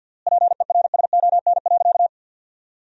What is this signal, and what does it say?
Morse code, heard as a keyed tone, about 53 wpm, 700 Hz. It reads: PERSON.
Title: 4x Word-Spacing - Sentences from Top 500 Words - 50wpm